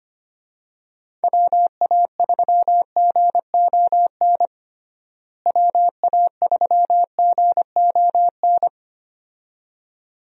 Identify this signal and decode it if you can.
Morse code, heard as a keyed tone, about 25 wpm, 700 Hz. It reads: WA3GON WA3GON
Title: USA Call Signs - 6 Characters - Repeated 2x - 25wpm